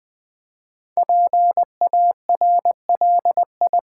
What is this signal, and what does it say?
Morse code, heard as a keyed tone, about 20 wpm, 700 Hz. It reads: PARLI
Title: Build A Word - 5 Letters At A Time - 10-Letter Words - 20wpm